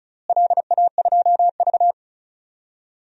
Morse code, heard as a keyed tone, about 34 wpm, 700 Hz. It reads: LA2V
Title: DXCC-100 Call Signs - 4 Characters - Repeated 3x - 35wpm